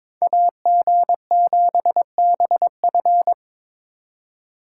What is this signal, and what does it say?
Morse code, heard as a keyed tone, about 26 wpm, 700 Hz. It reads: AG7BF
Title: Speed Racing - USA 2x2 Call Signs 22wpm